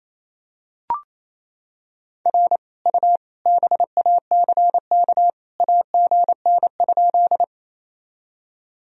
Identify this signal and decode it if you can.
Morse code, heard as a keyed tone, about 28 wpm, 700 Hz. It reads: R U BACK AGN?
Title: Sentences from Top 200 Words and Abbreviations 28wpm